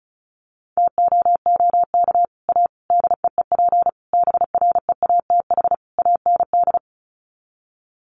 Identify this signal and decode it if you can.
Morse code, heard as a keyed tone, about 34 wpm, 700 Hz. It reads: TOOK A DEEP BREATH AND
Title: Sets of 5 Words Encore 35wpm